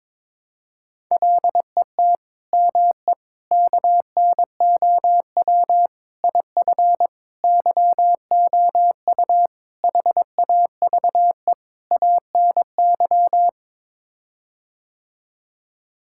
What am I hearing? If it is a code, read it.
Morse code, heard as a keyed tone, about 22 wpm, 700 Hz. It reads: LET ME KNOW IF YOU HAVE ANY
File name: practice / 22wpm / Sets of 7 Words - 22wpm